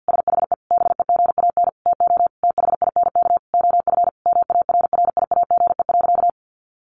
Morse code, heard as a keyed tone, about 50 wpm, 700 Hz. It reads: SHE BEGAN TO THINK OF MARRIAGE.
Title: Sentences from Top 800 Words 50wpm